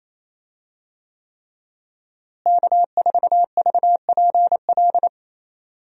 Morse code, heard as a keyed tone, about 28 wpm, 700 Hz. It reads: K4VPL
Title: USA Call Signs - 5 Characters - Repeated 3x - 28wpm